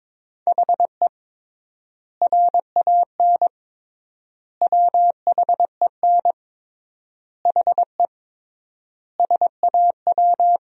Morse code, heard as a keyed tone, about 22 wpm, 700 Hz. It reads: HE RAN WHEN HE SAW
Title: 3x Word-Spacing - Sentences from Top 500 Words - 22wpm